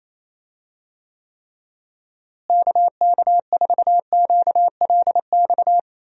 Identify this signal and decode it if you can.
Morse code, heard as a keyed tone, about 28 wpm, 700 Hz. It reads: KK4QLX
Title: USA Call Signs - 6 Characters - Repeated 3x - 28wpm